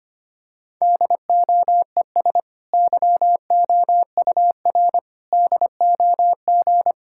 Morse code, heard as a keyed tone, about 25 wpm, 700 Hz. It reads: DOES YOUR DOG
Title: Sentences from Top 1600 Words 25wpm